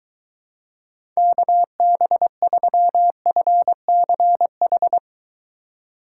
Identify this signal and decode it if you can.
Morse code, heard as a keyed tone, about 23 wpm, 700 Hz. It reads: KB3FCH